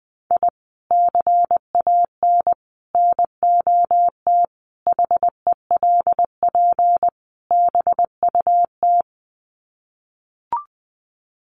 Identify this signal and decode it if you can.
Morse code, heard as a keyed tone, about 20 wpm, 700 Hz. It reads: I CAN NOT HELP BUT
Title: Sets of 5 Words Encore2 20wpm